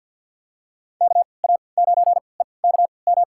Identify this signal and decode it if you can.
Morse code, heard as a keyed tone, about 52 wpm, spaced as for 29 wpm, 700 Hz. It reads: KA9EKM
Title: USA Call Signs - 3x Character Spacing - 50wpm